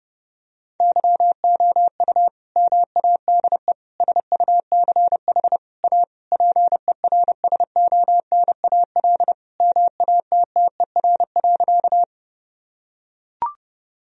Morse code, heard as a keyed tone, about 30 wpm, 700 Hz. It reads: YOU MADE SUCH A PERSONAL MATTER.
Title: Sentences from Top 1000 Words - Review Part 4 - 30wpm